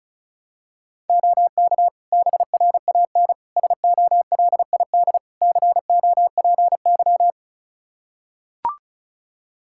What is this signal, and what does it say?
Morse code, heard as a keyed tone, about 34 wpm, 700 Hz. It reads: OK BRAN SOLID COPY
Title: Practice QSO - Exchange 3 - More Info - 35wpm